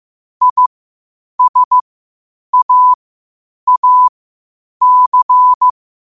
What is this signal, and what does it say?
Morse code, heard as a keyed tone, about 15 wpm, spaced as for 8 wpm, 1000 Hz. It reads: ISAAC
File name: qotdfiles / 8wpm